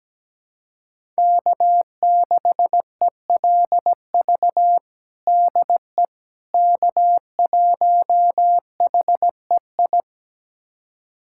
Morse code, heard as a keyed tone, about 17 wpm, 700 Hz. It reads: K6ELV DE K1HEI
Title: USA Callsign DE Callsign - 5 Characters - 17wpm